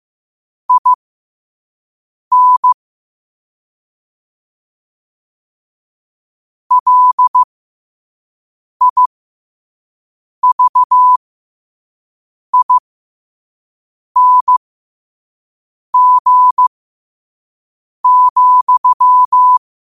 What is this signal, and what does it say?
Morse code, heard as a keyed tone, about 15 wpm, spaced as for 5 wpm, 1000 Hz. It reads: IN LIVING,
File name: 2024-10-07-5wpm